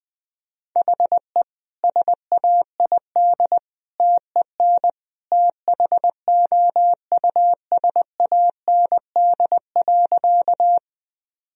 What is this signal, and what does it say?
Morse code, heard as a keyed tone, about 21 wpm, 700 Hz. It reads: HE SAID TEN THOUSAND.
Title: Sentences from Top 1000 Words - Review Part 3 - 20wpm